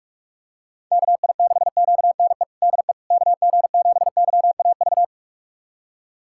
Morse code, heard as a keyed tone, about 46 wpm, 700 Hz. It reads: KI6QNE DE KG7YAV
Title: USA Callsign DE Callsign 45wpm